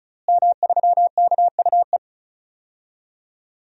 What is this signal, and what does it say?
Morse code, heard as a keyed tone, about 36 wpm, 700 Hz. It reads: M3KUE
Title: DXCC-100 Call Signs - 5 Characters - Repeated 2x - 35wpm